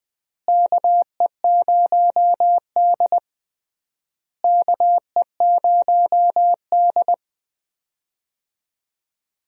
Morse code, heard as a keyed tone, about 20 wpm, 700 Hz. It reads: KE0D KE0D
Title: USA Call Signs - 4 Characters - Repeated 3x - 20wpm